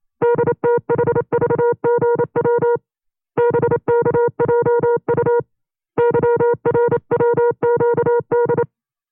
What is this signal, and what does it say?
Morse code, heard as a keyed tone, about 28 wpm, 480 Hz. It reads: DTHVGW BKJU YRWQD
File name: FreeFiles / CW-practice / 25WPM-side1